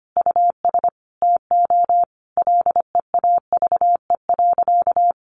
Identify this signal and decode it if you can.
Morse code, heard as a keyed tone, about 25 wpm, 700 Hz. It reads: US TO LEAVE.
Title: Sentences from Top 1500 Words 25wpm